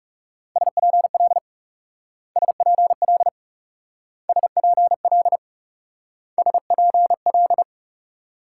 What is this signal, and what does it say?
Morse code, heard as a keyed tone, about 37 wpm, 700 Hz. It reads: SPL SPL SPL SPL
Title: Speed-Racing Multi-letter Phonemes - 30wpm